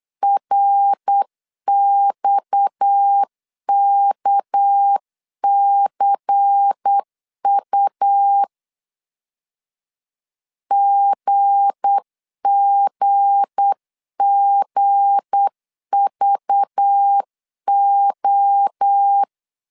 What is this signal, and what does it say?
Morse code, heard as a keyed tone, about 8 wpm, 800 Hz. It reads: RXKCU GGGVO
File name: LEZIONI / 7 GF9Z / 7 G 8wpm 3118